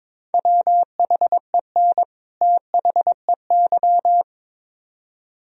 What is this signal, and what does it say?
Morse code, heard as a keyed tone, about 24 wpm, 700 Hz. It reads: WHEN THEY
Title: Speed-Racing - Sets of 2 Words 22wpm